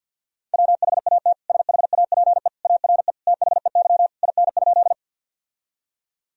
Morse code, heard as a keyed tone, about 50 wpm, 700 Hz. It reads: WHAT SHAPE ARE THEY IN?
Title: Sentences from Top 1000 Words - Review Part 3 - 50wpm